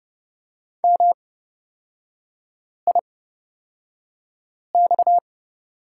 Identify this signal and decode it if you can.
Morse code, heard as a keyed tone, about 30 wpm, spaced as for 5 wpm, 700 Hz. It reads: MIX
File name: Lesson 37e - Letter x - Three Characters 30wpm